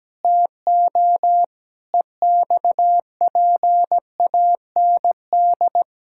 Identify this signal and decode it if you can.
Morse code, heard as a keyed tone, about 17 wpm, 700 Hz. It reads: TO EXPAND
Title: Pangram Challenge v2 - 17wpm